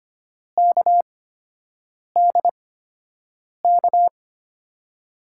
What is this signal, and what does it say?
Morse code, heard as a keyed tone, about 25 wpm, spaced as for 7 wpm, 700 Hz. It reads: KDK
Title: USA Call Signs - 5 Characters - 8x Character Spacing - 25wpm